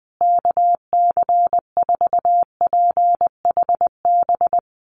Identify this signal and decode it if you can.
Morse code, heard as a keyed tone, about 20 wpm, 700 Hz. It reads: KC4PHB
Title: USA Call Signs Repeated 2x 20wpm